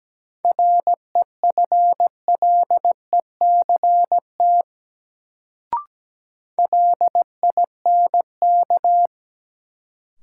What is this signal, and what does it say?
Morse code, heard as a keyed tone, about 17 wpm, 700 Hz. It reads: REFLECT LINK
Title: Sentences from Top 1500 Words - New Word Drill Alt - 17wpm